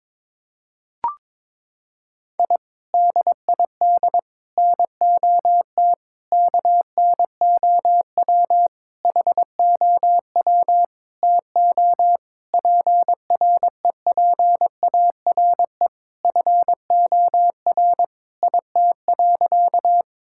Morse code, heard as a keyed tone, about 22 wpm, 700 Hz. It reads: I DID NOT KNOW HOW TO PREPARE FOR IT.
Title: Sentences from Top 1000 Words - Review Part 6 - 22wpm